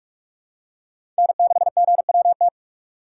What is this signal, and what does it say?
Morse code, heard as a keyed tone, about 46 wpm, 700 Hz. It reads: N6GWT
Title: USA Call Signs - 5 Characters - 2x Character Spacing - 45wpm